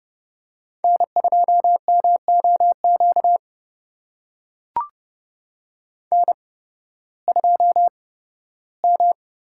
Morse code, heard as a keyed tone, about 30 wpm, spaced as for 16 wpm, 700 Hz. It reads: N2MOQ N2M
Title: USA Call Signs - 5 Characters - 8x Character Spacing - 30wpm